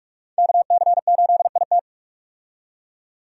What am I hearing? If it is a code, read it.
Morse code, heard as a keyed tone, about 43 wpm, 700 Hz. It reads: KC8IT
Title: USA Call Signs - 5 Characters - 45wpm